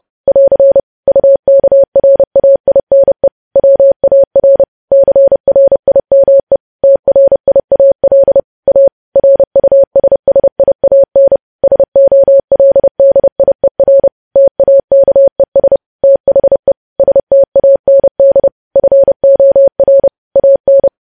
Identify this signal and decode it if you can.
Morse code, heard as a keyed tone, about 30 wpm, 550 Hz. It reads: <AR> UKRAINE WAR CRIME TRIAL A RUSSIAN SOLDIER TAKES THE STAND FOR AN
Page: news headlines in morse code at 30 wpm